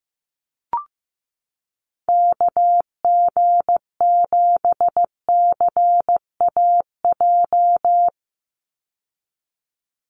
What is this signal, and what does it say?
Morse code, heard as a keyed tone, about 15 wpm, 700 Hz. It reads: KG7CAJ
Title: USA Call Signs 15wpm